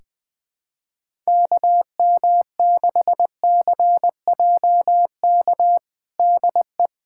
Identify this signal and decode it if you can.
Morse code, heard as a keyed tone, about 20 wpm, 700 Hz. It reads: KM6CJK DE
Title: USA Callsign DE Callsign - 6 Characters - 20wpm